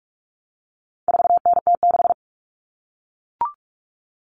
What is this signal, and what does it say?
Morse code, heard as a keyed tone, about 44 wpm, 700 Hz. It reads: 4NT6
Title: Common Cut Numbers - 4-Digit 45wpm